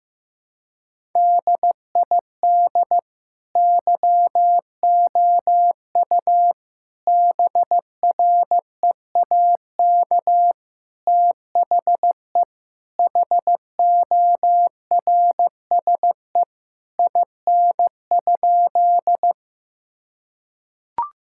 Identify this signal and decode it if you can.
Morse code, heard as a keyed tone, about 15 wpm, 700 Hz. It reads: DID YOU BREAK THE HORSE IN?
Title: Sentences from Top 1000 Words - Review Part 6 - 15wpm